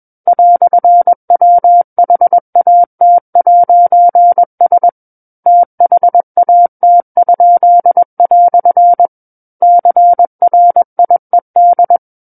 Morse code, heard as a keyed tone, about 21 wpm, 700 Hz. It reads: "WHAT'S THAT?" CRIED
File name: Book One / 12-Chapter 12 - 21 WPM